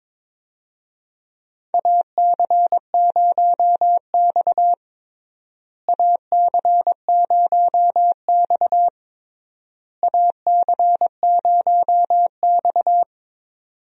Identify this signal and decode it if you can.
Morse code, heard as a keyed tone, about 22 wpm, 700 Hz. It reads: AC0X AC0X AC0X
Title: USA Call Signs - 4 Characters - Repeated 3x - 22wpm